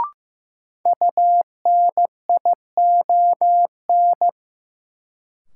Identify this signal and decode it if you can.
Morse code, heard as a keyed tone, about 15 wpm, 700 Hz. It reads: UNION
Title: Sentences from Top 1400 Words - New Word Drill - 15wpm